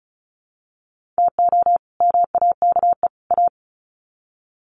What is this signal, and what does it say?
Morse code, heard as a keyed tone, about 34 wpm, 700 Hz. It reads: TO MAKE A
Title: Sets of 3 Words 35wpm